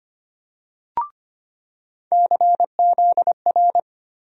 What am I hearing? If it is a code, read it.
Morse code, heard as a keyed tone, about 25 wpm, 700 Hz. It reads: CZR
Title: USA Call Sign Postfixes - 3 Letters 25wpm